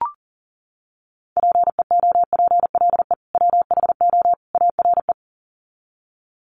Morse code, heard as a keyed tone, about 40 wpm, 700 Hz. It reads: PEOPLE WHO ARE
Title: Sets of 3 Words 40wpm